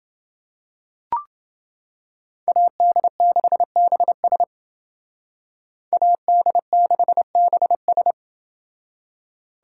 Morse code, heard as a keyed tone, about 28 wpm, 700 Hz. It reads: AD6BS AD6BS